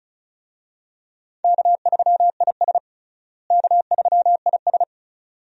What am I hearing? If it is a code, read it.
Morse code, heard as a keyed tone, about 36 wpm, 700 Hz. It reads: K3IS K3IS
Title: USA Call Signs - 4 Characters - Repeated 2x - 35wpm